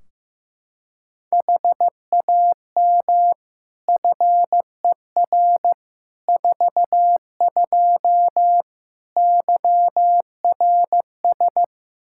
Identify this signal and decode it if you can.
Morse code, heard as a keyed tone, about 16 wpm, 700 Hz. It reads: HAM FER 42 YRS
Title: Practice QSO - Exchange 3 - More Info - 15wpm